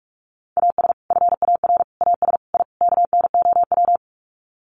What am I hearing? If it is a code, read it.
Morse code, heard as a keyed tone, about 44 wpm, 700 Hz. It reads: AS FAR AS I KNOW
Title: Sets of 5 Words 45wpm